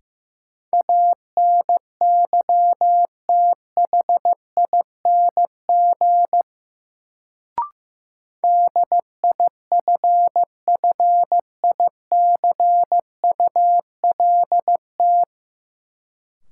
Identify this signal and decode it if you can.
Morse code, heard as a keyed tone, about 16 wpm, 700 Hz. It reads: ANYTHING DIFFICULT
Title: Sets of 4 Words - New Word Drill - 15wpm